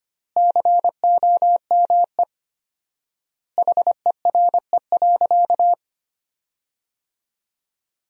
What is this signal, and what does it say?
Morse code, heard as a keyed tone, about 25 wpm, 700 Hz. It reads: COME HERE.
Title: 4x Word-Spacing - Sentences from Top 300 Words - 25wpm